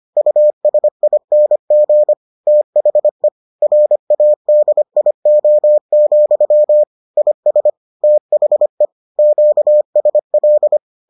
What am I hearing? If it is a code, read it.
Morse code, heard as a keyed tone, about 25 wpm, 600 Hz. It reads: USING THE RADIO, IS THE QSL